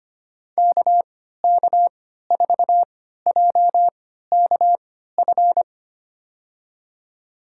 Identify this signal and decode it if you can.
Morse code, heard as a keyed tone, about 25 wpm, spaced as for 14 wpm, 700 Hz. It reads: KK4JKF
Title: USA Call Signs - 6 Characters - 3x Character Spacing - 25wpm